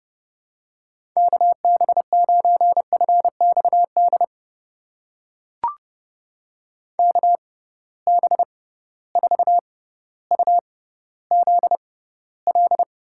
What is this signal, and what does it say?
Morse code, heard as a keyed tone, about 30 wpm, spaced as for 15 wpm, 700 Hz. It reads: KB9FXD KB4UZL
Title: USA Call Signs - 6 Characters - 6x Character Spacing - 30wpm